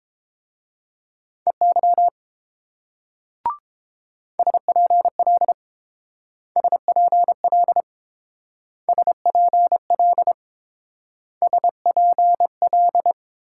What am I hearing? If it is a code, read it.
Morse code, heard as a keyed tone, about 28 wpm, 700 Hz. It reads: EY SPL SPL SPL SPL